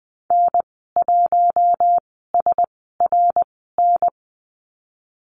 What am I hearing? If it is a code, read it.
Morse code, heard as a keyed tone, about 20 wpm, spaced as for 14 wpm, 700 Hz. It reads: N1SRN
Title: 062 Lesson 11g - Letter h - Call Signs 20wpm